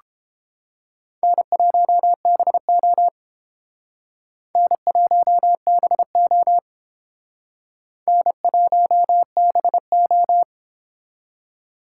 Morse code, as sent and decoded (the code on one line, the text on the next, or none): -. .---- -... --- / -. .---- -... --- / -. .---- -... ---
N1BO N1BO N1BO